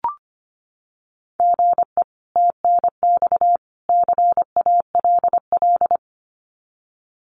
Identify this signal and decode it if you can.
Morse code, heard as a keyed tone, about 25 wpm, 700 Hz. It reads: GE TNX CALL
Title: Common QSO Phrases 25wpm